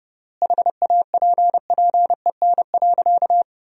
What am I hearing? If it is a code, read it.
Morse code, heard as a keyed tone, about 30 wpm, 700 Hz. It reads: HAPPEN.
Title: 3x Word-Spacing - Sentences from Top 400 Words - 30wpm